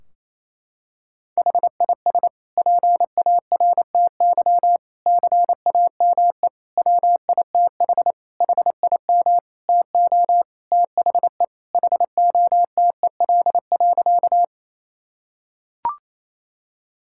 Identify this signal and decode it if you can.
Morse code, heard as a keyed tone, about 28 wpm, 700 Hz. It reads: HIS PARTY CAME WITH HIM TO THE HOTEL.
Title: Sentences from Top 1600 Words 28wpm